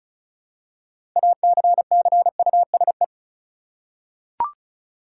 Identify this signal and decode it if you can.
Morse code, heard as a keyed tone, about 34 wpm, 700 Hz. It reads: ACCUSE
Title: Six Letter Words - 35wpm